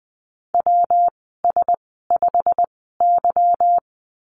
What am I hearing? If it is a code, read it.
Morse code, heard as a keyed tone, about 20 wpm, spaced as for 15 wpm, 700 Hz. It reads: WS5Y
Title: 161 Lesson 26g - Letter g - Call Signs 20wpm